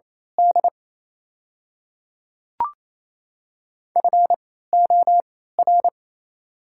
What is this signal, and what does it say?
Morse code, heard as a keyed tone, about 28 wpm, spaced as for 16 wpm, 700 Hz. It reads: D FOR
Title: 3x Character Spacing - Top 100 Words - 28wpm